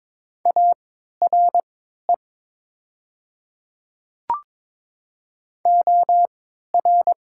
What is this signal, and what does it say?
Morse code, heard as a keyed tone, about 22 wpm, spaced as for 12 wpm, 700 Hz. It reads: ARE OR
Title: 3x Character Spacing - Top 100 Words - 22wpm